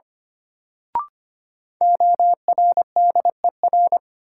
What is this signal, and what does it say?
Morse code, heard as a keyed tone, about 25 wpm, 700 Hz. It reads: ORDER
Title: Sets of 2 Words - New Word Drill - 25wpm